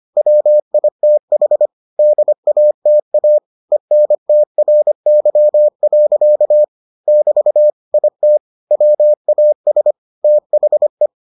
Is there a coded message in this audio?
Yes: Morse code from 0.2 to 11.1 s, heard as a keyed tone, about 25 wpm, 600 Hz. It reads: WITH DATA ENTRY. <BT> IT WAS THE